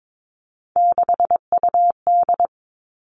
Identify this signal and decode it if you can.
Morse code, heard as a keyed tone, about 22 wpm, 700 Hz. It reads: 6UD
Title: Three Challenging Letter-Number Combinations 22wpm